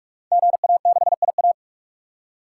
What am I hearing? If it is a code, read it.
Morse code, heard as a keyed tone, about 44 wpm, 700 Hz. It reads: GA6IA
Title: Code Groups - 5 Letter:Numbers - 45wpm